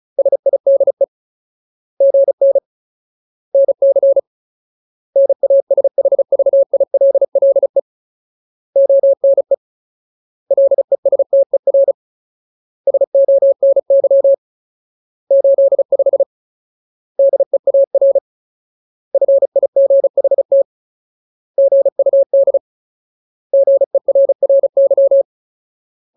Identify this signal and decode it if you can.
Morse code, heard as a keyed tone, about 34 wpm, 550 Hz. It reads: SIDE GN NC NASHVILLE ONE LESTER SONY 85 DEAR FIGHT GUD GERRY
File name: Ultimate Word List / 35WPM Ultimate